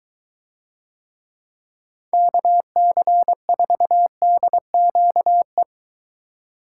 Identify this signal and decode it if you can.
Morse code, heard as a keyed tone, about 22 wpm, 700 Hz. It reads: KC4DQE